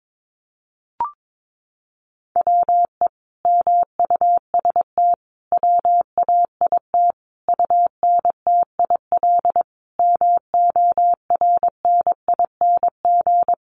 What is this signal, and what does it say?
Morse code, heard as a keyed tone, about 22 wpm, 700 Hz. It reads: WE MUST WAIT UNTIL MORNING
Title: Sentences from Top 1300 Words 22wpm